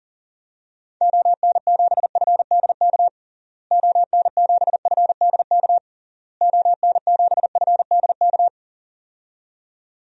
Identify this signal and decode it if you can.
Morse code, heard as a keyed tone, about 37 wpm, 700 Hz. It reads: ON7FDK ON7FDK ON7FDK
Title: DXCC-100 Call Signs - 6 Characters - Repeated 3x - 40wpm